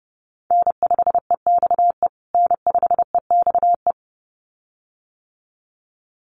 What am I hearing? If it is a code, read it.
Morse code, heard as a keyed tone, about 30 wpm, 700 Hz. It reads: N5EXE N5EXE
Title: USA Call Signs Repeated 2x 30wpm